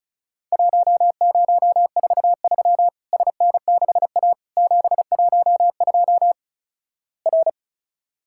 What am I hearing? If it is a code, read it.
Morse code, heard as a keyed tone, about 36 wpm, 700 Hz. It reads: 1043 SN6A 712 R
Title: Worked All Europe DX Contest - QTC Exchange - 35wpm